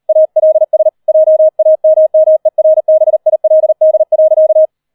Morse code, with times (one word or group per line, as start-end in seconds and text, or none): 0.1-0.9 ALS
1.1-4.7 JAMMERBILD.